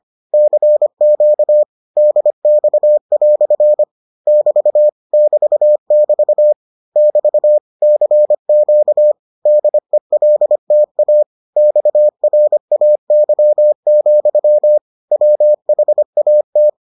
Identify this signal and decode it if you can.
Morse code, heard as a keyed tone, about 25 wpm, 600 Hz. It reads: CQ DX" <BT> <BT><BT> <BT> CQ DELTA XRAY, WHAT